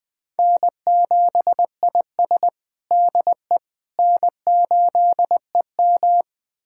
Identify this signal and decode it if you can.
Morse code, heard as a keyed tone, about 20 wpm, 700 Hz. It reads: N7IS DE N8EM
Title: USA Callsign DE Callsign - 4 Characters - 20wpm